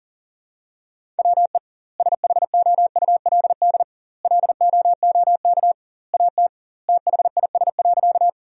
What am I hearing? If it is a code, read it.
Morse code, heard as a keyed tone, about 40 wpm, 700 Hz. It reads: WE SHOULD LOOK AT THIS.
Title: 2x Word-Spacing - Sentences from Top 300 Words - 40wpm